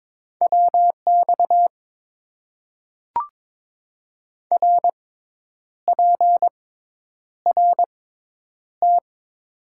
Morse code, heard as a keyed tone, about 22 wpm, spaced as for 9 wpm, 700 Hz. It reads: WX RPRT
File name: Top 90 QSO Elements - 6x Character Spacing - 22wpm